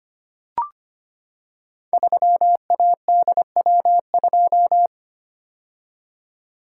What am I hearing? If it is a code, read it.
Morse code, heard as a keyed tone, about 25 wpm, 700 Hz. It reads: 3ADW2
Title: Code Groups - 5 Letter:Numbers - 25wpm